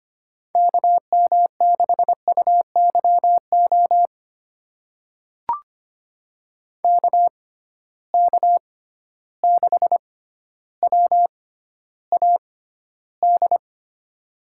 Morse code, heard as a keyed tone, about 25 wpm, spaced as for 13 wpm, 700 Hz. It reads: KM6UYO KK6WAD
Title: USA Call Signs - 6 Characters - 6x Character Spacing - 25wpm